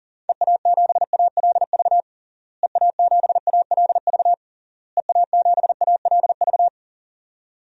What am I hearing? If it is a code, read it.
Morse code, heard as a keyed tone, about 40 wpm, 700 Hz. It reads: EA7ALV EA7ALV EA7ALV
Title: DXCC-100 Call Signs - 6 Characters - Repeated 3x - 40wpm